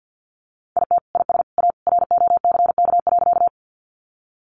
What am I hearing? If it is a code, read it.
Morse code, heard as a keyed tone, about 50 wpm, 700 Hz. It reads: IT IS A ROCK.
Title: Sentences from Top 400 Words 50wpm